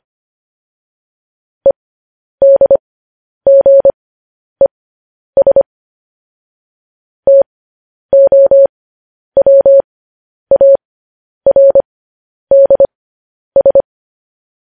Morse code, heard as a keyed tone, about 25 wpm, spaced as for 10 wpm, 550 Hz. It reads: EDGES TOWARDS
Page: news headlines in morse code at 10 wpm